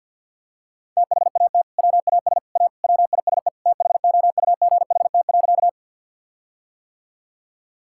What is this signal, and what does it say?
Morse code, heard as a keyed tone, about 50 wpm, 700 Hz. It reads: THAT WAS A WISE THOUGHT.
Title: Sentences from Top 1600 Words 50wpm